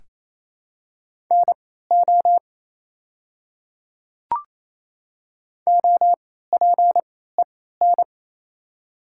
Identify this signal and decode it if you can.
Morse code, heard as a keyed tone, about 28 wpm, spaced as for 16 wpm, 700 Hz. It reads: NO OPEN
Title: 3x Character Spacing - Top 300 Words - 28wpm